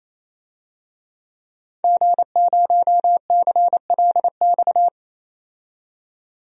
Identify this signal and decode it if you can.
Morse code, heard as a keyed tone, about 28 wpm, 700 Hz. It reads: G0CLX
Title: DXCC-100 Call Signs - 5 Characters - 28wpm